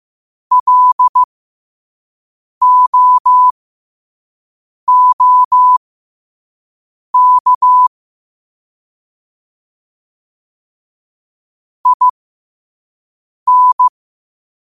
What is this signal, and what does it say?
Morse code, heard as a keyed tone, about 15 wpm, spaced as for 5 wpm, 1000 Hz. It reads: LOOK IN
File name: qotdfiles / 5wpm